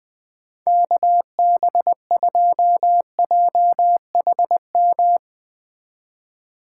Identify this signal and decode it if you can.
Morse code, heard as a keyed tone, about 20 wpm, 700 Hz. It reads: KB2JHM
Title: USA Call Signs - 6 Characters - 4x Character Spacing - 20wpm